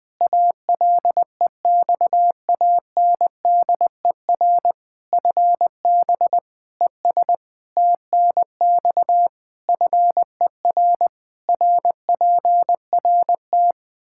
Morse code, heard as a keyed tone, about 20 wpm, 700 Hz. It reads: ALEXANDER FB ES TNX FER RPRT
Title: K1USN SST QSO Practice 20wpm